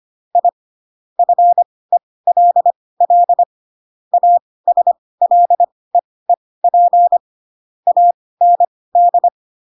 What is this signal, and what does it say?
Morse code, heard as a keyed tone, about 25 wpm, spaced as for 18 wpm, 700 Hz. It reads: I FELL ASLEEP AND
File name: Pinocchio / Pinocchio ch7 12 25 ik1hge